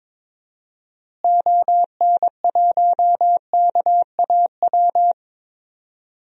Morse code, heard as a keyed tone, about 22 wpm, 700 Hz. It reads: ON1KAW
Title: DXCC-100 Call Signs - 6 Characters - Repeated 3x - 22wpm